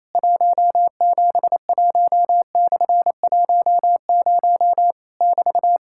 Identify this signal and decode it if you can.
Morse code, heard as a keyed tone, about 28 wpm, 700 Hz. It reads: 171/10 <BT>
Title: Worked All Europe DX Contest - QTC Exchange - 28wpm